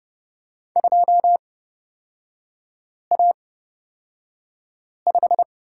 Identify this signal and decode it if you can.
Morse code, heard as a keyed tone, about 31 wpm, spaced as for 5 wpm, 700 Hz. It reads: 2A5
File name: Lesson 38e - Letter q - Three Characters 30wpm